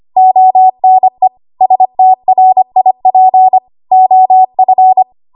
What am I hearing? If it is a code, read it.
Morse code, heard as a keyed tone, about 25 wpm, 750 Hz. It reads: ONE STRIP OF
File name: Morse / 25WPM